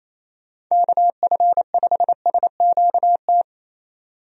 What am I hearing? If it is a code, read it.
Morse code, heard as a keyed tone, about 28 wpm, 700 Hz. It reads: KF5SQT
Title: USA Call Signs - 6 Characters - 28wpm